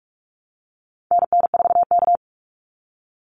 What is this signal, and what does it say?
Morse code, heard as a keyed tone, about 44 wpm, 700 Hz. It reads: NN4K
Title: Call Signs Repeated 3x - 45wpm